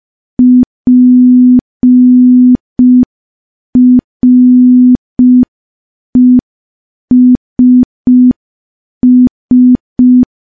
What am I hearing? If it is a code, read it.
Morse code, heard as a keyed tone, about 5 wpm, 250 Hz. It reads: PRESS
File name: morse / morse-training-05WPM